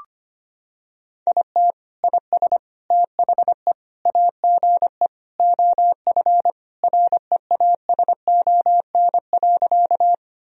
Morse code, heard as a keyed tone, about 25 wpm, 700 Hz. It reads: IT IS THE AGE OF REASON.
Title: Sentences from Top 1000 Words - Review Part 3 - 25wpm